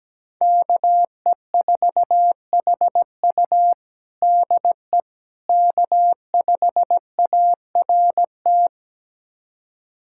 Morse code, heard as a keyed tone, about 17 wpm, 700 Hz. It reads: KE4HU DE K5ART
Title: USA Callsign DE Callsign - 5 Characters - 17wpm